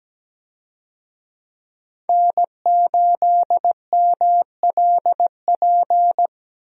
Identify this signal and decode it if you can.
Morse code, heard as a keyed tone, about 17 wpm, 700 Hz. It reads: N8MLP
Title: USA Call Signs - 5 Characters - 17wpm